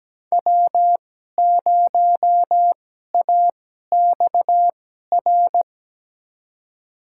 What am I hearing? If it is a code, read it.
Morse code, heard as a keyed tone, about 17 wpm, spaced as for 12 wpm, 700 Hz. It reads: W0AXR
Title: USA Call Signs - 5 Characters - 2x Character Spacing - 17wpm